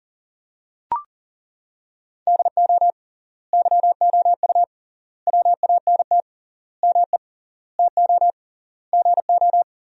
Morse code, heard as a keyed tone, about 40 wpm, 700 Hz. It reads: DO YOU WANT ME TO GO
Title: 3x Word-Spacing - Sentences from Top 200 Words - 40wpm